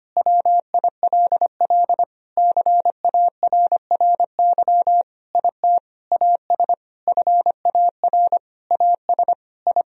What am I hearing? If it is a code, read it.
Morse code, heard as a keyed tone, about 25 wpm, 700 Hz. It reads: WILL CARRY IT AS FAR AS I
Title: Sentences from Top 1000 Words - Review Part 1 - 25wpm